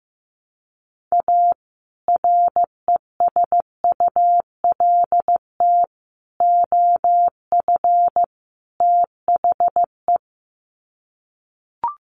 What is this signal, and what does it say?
Morse code, heard as a keyed tone, about 15 wpm, 700 Hz. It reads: A RESULT OF THE
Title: Sets of 4 Words 15wpm